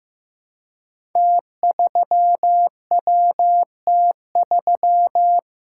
Morse code, heard as a keyed tone, about 15 wpm, 700 Hz. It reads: T3WT3
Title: Code Groups - 5 Letter:Numbers - 15wpm